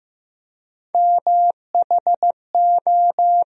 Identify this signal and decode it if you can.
Morse code, heard as a keyed tone, about 15 wpm, 700 Hz. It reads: MHO
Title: USA Call Sign Postfixes - 3 Letters 15wpm